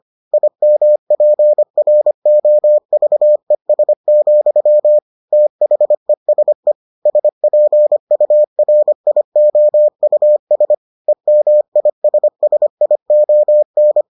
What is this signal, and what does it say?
Morse code, heard as a keyed tone, about 25 wpm, 600 Hz. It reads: IMPROVES, THESE SPURIOUS EMISSION